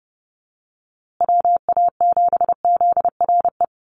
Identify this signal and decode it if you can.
Morse code, heard as a keyed tone, about 30 wpm, 700 Hz. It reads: WA7ZRE
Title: USA Call Signs Repeated 2x 30wpm